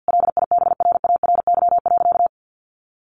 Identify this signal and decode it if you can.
Morse code, heard as a keyed tone, about 50 wpm, 700 Hz. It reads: LIBRARY.
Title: Sentences from Top 1300 Words 50wpm